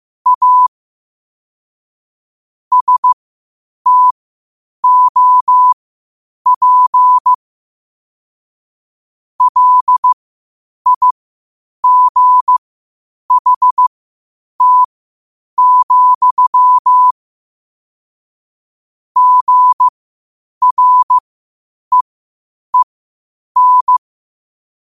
Morse code, heard as a keyed tone, about 15 wpm, spaced as for 8 wpm, 1000 Hz. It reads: A STOP LIGHT, GREEN
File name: qotdfiles / 8wpm